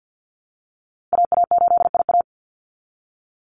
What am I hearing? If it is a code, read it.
Morse code, heard as a keyed tone, about 50 wpm, 700 Hz. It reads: AA8IA
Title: Call Signs Repeated 2x - 50wpm